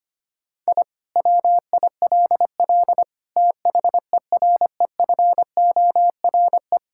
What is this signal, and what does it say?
Morse code, heard as a keyed tone, about 25 wpm, 700 Hz. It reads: I WILL THEREFORE
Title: Sentences from Top 1000 Words - Review Part 5 - 25wpm